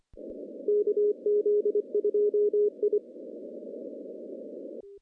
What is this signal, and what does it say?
Morse code, heard as a keyed tone, about 23 wpm, 420 Hz. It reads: KZ2I
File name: contests / CW-SS-k5tr- -1400